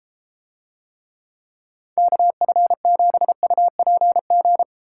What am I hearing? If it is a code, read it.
Morse code, heard as a keyed tone, about 33 wpm, 700 Hz. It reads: KF7UPG